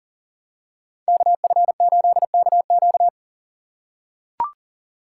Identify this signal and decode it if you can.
Morse code, heard as a keyed tone, about 40 wpm, 700 Hz. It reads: KF8KQ
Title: USA Call Signs - 2x Character Spacing - 40wpm